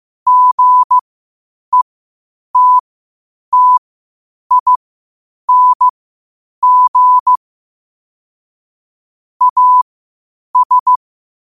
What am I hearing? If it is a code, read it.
Morse code, heard as a keyed tone, about 15 wpm, spaced as for 8 wpm, 1000 Hz. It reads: GETTING AS
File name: qotdfiles / 8wpm